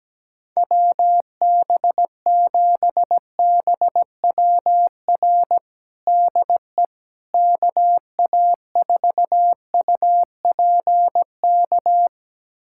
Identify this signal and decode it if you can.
Morse code, heard as a keyed tone, about 17 wpm, 700 Hz. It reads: WB7BWR DE KA4UPK
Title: USA Callsign DE Callsign 17wpm